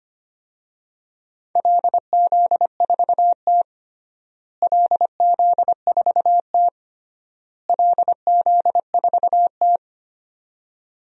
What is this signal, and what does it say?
Morse code, heard as a keyed tone, about 25 wpm, 700 Hz. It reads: LZ4T LZ4T LZ4T
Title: DXCC-100 Call Signs - 4 Characters - Repeated 3x - 25wpm